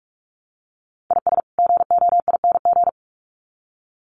Morse code, heard as a keyed tone, about 44 wpm, 700 Hz. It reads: IS GOING
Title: Sets of 2 Words 45wpm